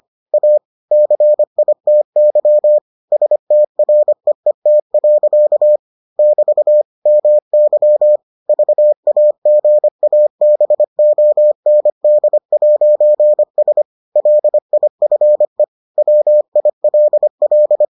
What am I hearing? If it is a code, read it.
Morse code, heard as a keyed tone, about 25 wpm, 600 Hz. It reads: A CITY STREET. <BT> MY VAGABOND'S LIFE WILL